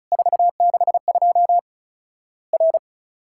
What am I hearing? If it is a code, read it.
Morse code, heard as a keyed tone, about 35 wpm, 700 Hz. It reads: 462 R
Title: Worked All Europe DX Contest - QTC Exchange - 35wpm